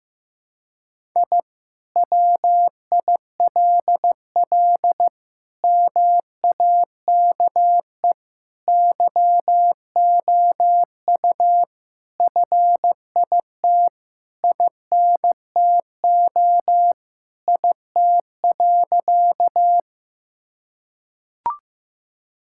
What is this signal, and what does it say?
Morse code, heard as a keyed tone, about 16 wpm, 700 Hz. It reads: I WILL MAKE YOU FIT INTO IT.
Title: Sentences from Top 1000 Words - Review Part 5 - 15wpm